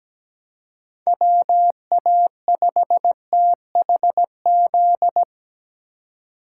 Morse code, heard as a keyed tone, about 20 wpm, 700 Hz. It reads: WA5THZ
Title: Speed Racing - USA 2x3 Call Signs 17wpm